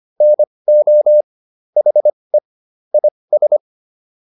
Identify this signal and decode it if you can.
Morse code, heard as a keyed tone, about 25 wpm, spaced as for 20 wpm, 600 Hz. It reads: NO HE IS